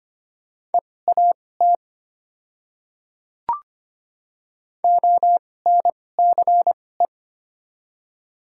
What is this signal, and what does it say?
Morse code, heard as a keyed tone, about 25 wpm, spaced as for 18 wpm, 690 Hz. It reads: EAT ONCE
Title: 2x Character Spacing - Top 300 Words - 25wpm